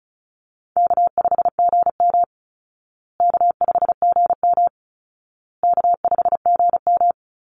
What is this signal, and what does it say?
Morse code, heard as a keyed tone, about 34 wpm, 700 Hz. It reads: K5GM K5GM K5GM
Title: Easy Call Signs Repeated 3x - 35wpm